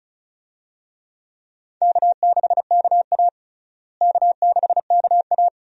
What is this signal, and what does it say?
Morse code, heard as a keyed tone, about 36 wpm, 700 Hz. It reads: K6KA K6KA
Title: USA Call Signs - 4 Characters - Repeated 2x - 35wpm